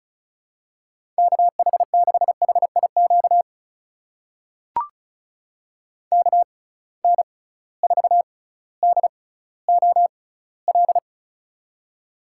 Morse code, heard as a keyed tone, about 34 wpm, spaced as for 18 wpm, 700 Hz. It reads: KH6HIQ KN4DOL
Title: USA Call Signs - 6 Characters - 6x Character Spacing - 35wpm